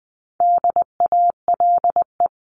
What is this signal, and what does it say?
Morse code, heard as a keyed tone, about 20 wpm, 700 Hz. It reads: DALE
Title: CWT Exchange 20wpm